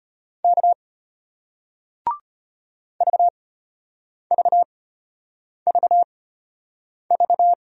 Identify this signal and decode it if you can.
Morse code, heard as a keyed tone, about 33 wpm, spaced as for 8 wpm, 700 Hz. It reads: K VVVV